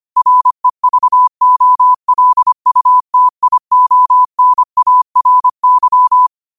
Morse code, heard as a keyed tone, about 25 wpm, 1000 Hz. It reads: REVOLUTIONARY
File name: qotdfiles / 25wpm